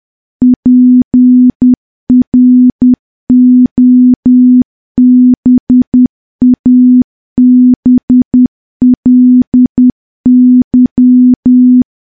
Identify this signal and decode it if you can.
Morse code, heard as a keyed tone, about 10 wpm, 250 Hz. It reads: PROBABLY
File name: morse / morse-training-10WPM